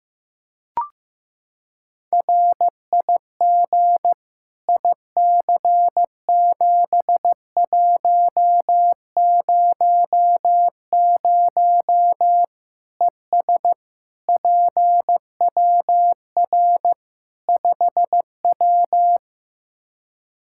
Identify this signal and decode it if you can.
Morse code, heard as a keyed tone, about 15 wpm, 700 Hz. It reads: RIG IC7100 ES PWR 5W
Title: Practice QSO - Exchange 2 - Info - 15wpm